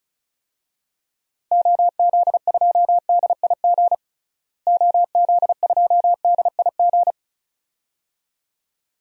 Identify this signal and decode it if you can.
Morse code, heard as a keyed tone, about 36 wpm, 700 Hz. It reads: OZ2DIG OZ2DIG
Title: DXCC-100 Call Signs - 6 Characters - Repeated 2x - 35wpm